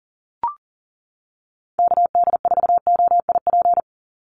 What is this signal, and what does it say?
Morse code, heard as a keyed tone, about 40 wpm, 700 Hz. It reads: KD4OIP
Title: USA Call Signs 40wpm